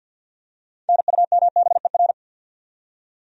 Morse code, heard as a keyed tone, about 53 wpm, 700 Hz. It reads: NUMBER
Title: Top 100 Words 50wpm